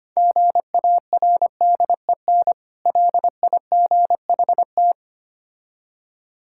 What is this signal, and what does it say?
Morse code, heard as a keyed tone, about 25 wpm, 700 Hz. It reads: GARDEN LIGHT
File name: practice / 25wpm / Difficult Words - Speed-Racing with Hints - 25wpm